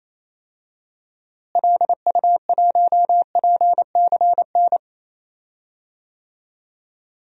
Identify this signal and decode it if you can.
Morse code, heard as a keyed tone, about 28 wpm, 700 Hz. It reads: LU1PCN
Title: DXCC-100 Call Signs - 6 Characters - Repeated 3x - 28wpm